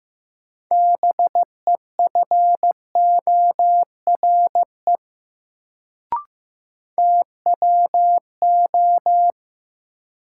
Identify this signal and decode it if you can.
Morse code, heard as a keyed tone, about 15 wpm, 700 Hz. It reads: BEFORE TWO
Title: Sets of 2 Words - New Word Drill - 15wpm